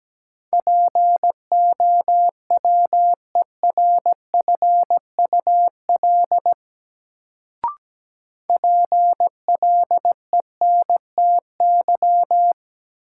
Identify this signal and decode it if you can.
Morse code, heard as a keyed tone, about 17 wpm, 700 Hz. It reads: POWERFUL PLENTY
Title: Sentences from Top 1300 Words - New Word Drill Alt - 17wpm